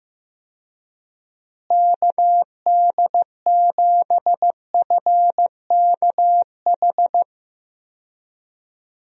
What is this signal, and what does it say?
Morse code, heard as a keyed tone, about 16 wpm, 700 Hz. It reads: KD7FKH
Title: USA Call Signs - 6 Characters - Repeated 2x - 15wpm